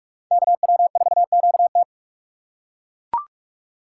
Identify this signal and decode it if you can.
Morse code, heard as a keyed tone, about 42 wpm, 700 Hz. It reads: KW4QT